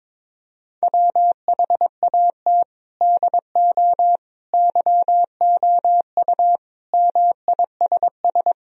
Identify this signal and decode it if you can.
Morse code, heard as a keyed tone, about 22 wpm, 700 Hz. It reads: WHAT DO YOU MISS
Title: Sentences from Top 1000 Words - Review Part 3 - 22wpm